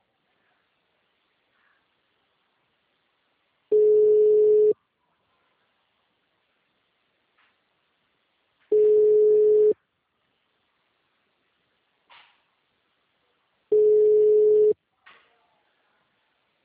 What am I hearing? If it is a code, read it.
Ringback tone (Europe, 425 Hz).
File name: out- -2002- - - .10